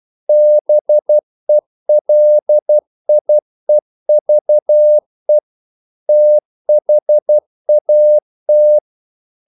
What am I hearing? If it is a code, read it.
Morse code, heard as a keyed tone, about 12 wpm, 600 Hz. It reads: BELIEVE THAT
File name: CW / Practice / EnglishQuotations0112